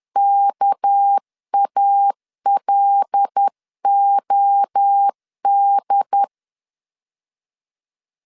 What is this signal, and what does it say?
Morse code, heard as a keyed tone, about 10 wpm, 800 Hz. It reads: KALOD
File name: LEZIONI / 9 2867 10wpm 2733